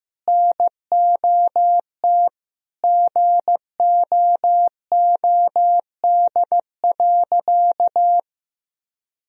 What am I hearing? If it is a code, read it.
Morse code, heard as a keyed tone, about 15 wpm, 700 Hz. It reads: NOT GOOD.
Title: Sentences from Top 1000 Words - Review Part 1 - 15wpm